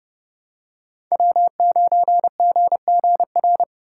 Morse code, heard as a keyed tone, about 30 wpm, 700 Hz. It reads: W9GGR
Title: USA Call Signs - 5 Characters - 4x Character Spacing - 30wpm